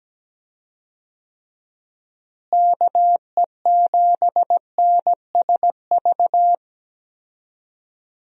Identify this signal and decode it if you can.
Morse code, heard as a keyed tone, about 17 wpm, 700 Hz. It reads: KE7NSV
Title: USA Call Signs - 6 Characters - Repeated 2x - 17wpm